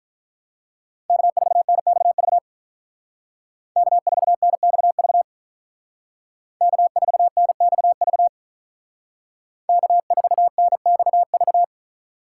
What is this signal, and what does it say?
Morse code, heard as a keyed tone, about 43 wpm, 700 Hz. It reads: K4NXV K4NXV K4NXV K4NXV
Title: Speed Racing - USA 1x3 Call Signs 35wpm